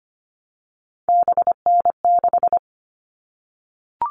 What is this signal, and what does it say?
Morse code, heard as a keyed tone, about 25 wpm, 700 Hz. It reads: BN6
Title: All Cut Numbers - 3-Digit 25wpm